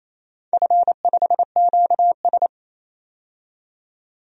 Morse code, heard as a keyed tone, about 28 wpm, 690 Hz. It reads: F5QS
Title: DXCC-100 Call Signs - 4 Characters - 28wpm